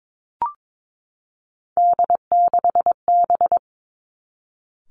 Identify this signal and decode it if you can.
Morse code, heard as a keyed tone, about 22 wpm, 700 Hz. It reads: D6B
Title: Three Challenging Letter-Number Combinations 22wpm